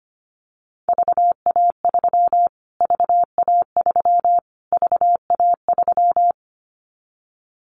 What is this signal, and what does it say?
Morse code, heard as a keyed tone, about 25 wpm, 700 Hz. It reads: VA3 VA3 VA3
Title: Callsign Prefixes for Top 25 DXCC - Repeated 3x - 25wpm